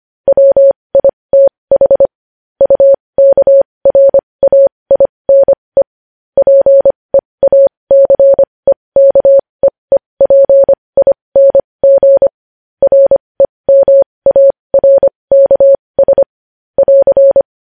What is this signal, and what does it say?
Morse code, heard as a keyed tone, about 25 wpm, spaced as for 20 wpm, 550 Hz. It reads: WITH UKRAINE PEACEKEEPING REMARKS <AR>
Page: News Headlines in Morse Code at 20 WPM